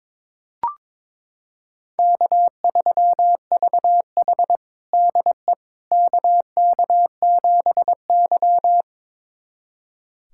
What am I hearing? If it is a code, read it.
Morse code, heard as a keyed tone, about 21 wpm, 700 Hz. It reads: K3VH DE KK7Y
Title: USA Callsign DE Callsign - 4 Characters - 22wpm